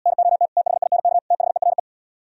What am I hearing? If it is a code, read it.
Morse code, heard as a keyed tone, about 77 wpm, 700 Hz. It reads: NOT THINK THERE